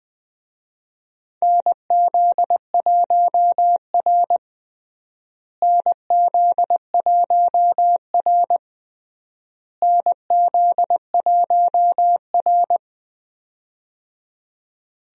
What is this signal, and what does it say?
Morse code, heard as a keyed tone, about 20 wpm, 700 Hz. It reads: NZ1R NZ1R NZ1R
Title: USA Call Signs - 4 Characters - Repeated 3x - 20wpm